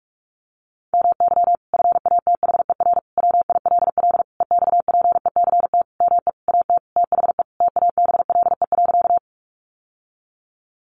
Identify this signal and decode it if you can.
Morse code, heard as a keyed tone, about 44 wpm, 700 Hz. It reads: MY FATHER WILL EXPECT ME AT THE TABLE.
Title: Sentences from Top 1100 Words 45wpm